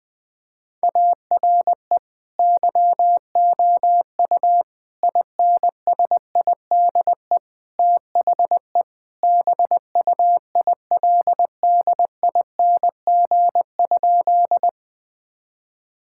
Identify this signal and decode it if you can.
Morse code, heard as a keyed tone, about 20 wpm, 700 Hz. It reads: ARE YOU INSIDE THE BUILDING?
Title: Sentences from Top 1000 Words - Review Part 6 - 20wpm